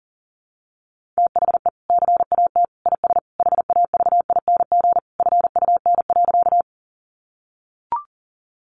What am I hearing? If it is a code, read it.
Morse code, heard as a keyed tone, about 40 wpm, 700 Hz. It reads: THE CAT IS HAVING FUN.
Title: Easy Phrases 40wpm